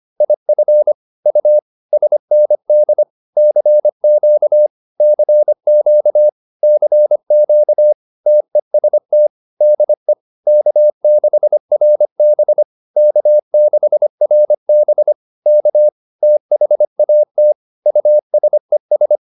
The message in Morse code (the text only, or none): IF U SND CQ CQ CQ TEST DE K6RB K6RB K THAT USES